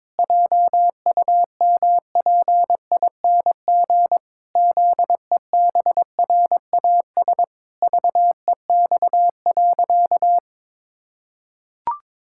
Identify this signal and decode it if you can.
Morse code, heard as a keyed tone, about 22 wpm, 700 Hz. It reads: JUMPING ZEBRAS VEX.
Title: Pangram Challenge v2 - 22wpm